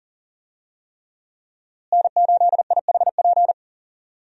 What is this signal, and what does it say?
Morse code, heard as a keyed tone, about 39 wpm, 700 Hz. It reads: N8IHP
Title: USA Call Signs - 5 Characters - Repeated 3x - 40wpm